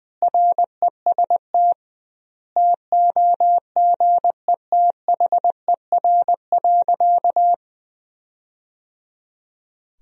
Morse code, heard as a keyed tone, about 21 wpm, 700 Hz. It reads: REST TOGETHER.
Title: 2x Word-Spacing - Sentences from Top 500 Words - 20wpm